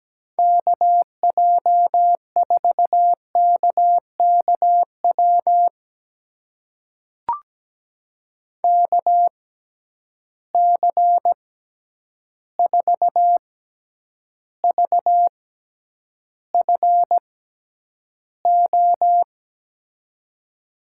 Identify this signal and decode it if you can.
Morse code, heard as a keyed tone, about 17 wpm, spaced as for 9 wpm, 700 Hz. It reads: KJ4KKW KC4VFO
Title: USA Call Signs - 6 Characters - 6x Character Spacing - 17wpm